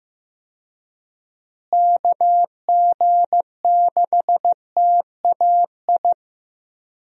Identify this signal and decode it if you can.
Morse code, heard as a keyed tone, about 15 wpm, 700 Hz. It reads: KG6TAI
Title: USA Call Signs - 6 Characters - Repeated 2x - 15wpm